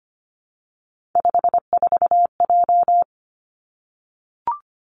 Morse code, heard as a keyed tone, about 25 wpm, 700 Hz. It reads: 54J
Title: Three Challenging Letter-Number Combinations 25wpm